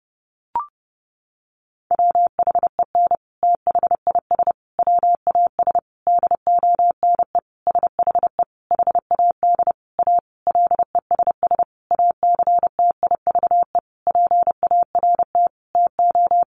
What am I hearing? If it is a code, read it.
Morse code, heard as a keyed tone, about 30 wpm, 700 Hz. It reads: WHEN THIS WAS DONE SHE HAD A LESS ACTIVE PART TO
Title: Sentences from Top 900 Words 30wpm